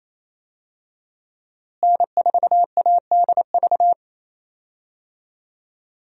Morse code, heard as a keyed tone, about 28 wpm, 700 Hz. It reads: N4ADV
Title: USA Call Signs - 5 Characters - Repeated 3x - 28wpm